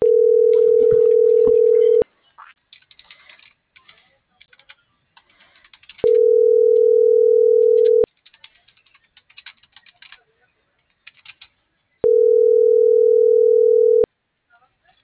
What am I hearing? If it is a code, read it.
Ringback tone (North America), 440 + 480 Hz.